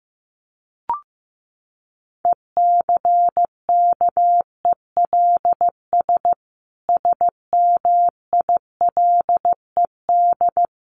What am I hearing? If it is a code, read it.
Morse code, heard as a keyed tone, about 15 wpm, 700 Hz. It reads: ECKELS SMILED
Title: A Sound Of Thunder 15wpm